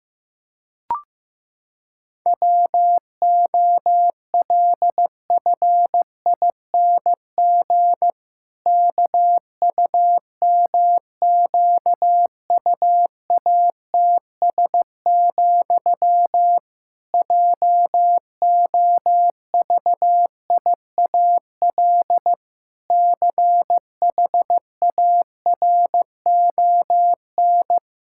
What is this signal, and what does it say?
Morse code, heard as a keyed tone, about 16 wpm, 700 Hz. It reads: WOLFING KUMQUATS, JOVIAL CHARON
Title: Pangram Challenge v2 - 15wpm